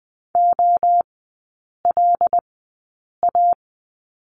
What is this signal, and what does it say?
Morse code, heard as a keyed tone, about 20 wpm, spaced as for 8 wpm, 700 Hz. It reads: OLA
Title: 100 Lesson 17e - Letter u - Three Characters 20wpm